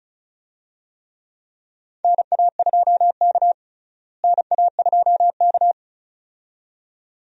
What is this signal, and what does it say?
Morse code, heard as a keyed tone, about 36 wpm, 700 Hz. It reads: NA2K NA2K
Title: USA Call Signs - 4 Characters - Repeated 2x - 35wpm